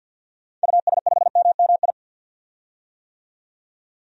Morse code, heard as a keyed tone, about 52 wpm, 700 Hz. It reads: US5MMI
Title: DXCC-100 Call Signs - 6 Characters - 50wpm